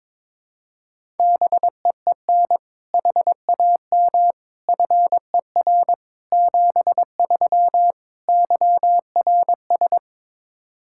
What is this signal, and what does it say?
Morse code, heard as a keyed tone, about 22 wpm, 700 Hz. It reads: BEEN HAM FER 73 YRS
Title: K1USN SST QSO Practice 22wpm